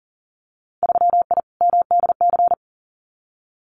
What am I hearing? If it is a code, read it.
Morse code, heard as a keyed tone, about 40 wpm, 700 Hz. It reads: 3I MDC
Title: Winter Field Day Exchanges Encore 40wpm